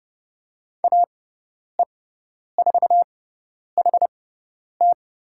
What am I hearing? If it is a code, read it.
Morse code, heard as a keyed tone, about 30 wpm, spaced as for 10 wpm, 690 Hz. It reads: AE4HT
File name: Lesson 17g - Letter u - Call Signs 30wpm